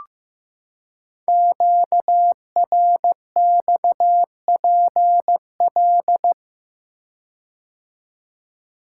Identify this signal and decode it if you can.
Morse code, heard as a keyed tone, about 15 wpm, 700 Hz. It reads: QRXPL
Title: Code Groups - 5 Letters - 15wpm